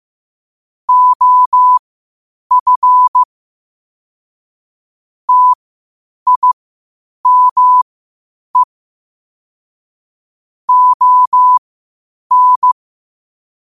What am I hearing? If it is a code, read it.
Morse code, heard as a keyed tone, about 15 wpm, spaced as for 8 wpm, 1000 Hz. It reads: OF TIME ON
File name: qotdfiles / 8wpm